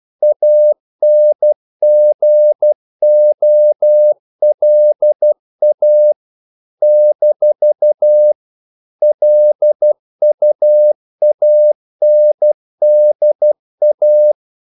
Morse code, heard as a keyed tone, about 12 wpm, 600 Hz. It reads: ANGOLA - LUANDA
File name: CW / Practice / NationalCapitals12